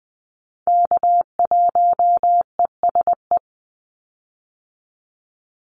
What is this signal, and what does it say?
Morse code, heard as a keyed tone, about 20 wpm, 700 Hz. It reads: K1ESE
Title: Call Signs - 20wpm